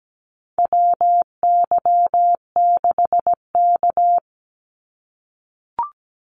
Morse code, heard as a keyed tone, about 17 wpm, 700 Hz. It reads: WY6K
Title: Call Signs - 17wpm